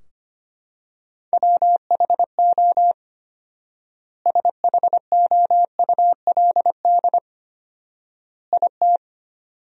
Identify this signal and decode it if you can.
Morse code, heard as a keyed tone, about 25 wpm, 700 Hz. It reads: WHO SHOULD IT
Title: 4x Word-Spacing - Sentences from Top 300 Words - 25wpm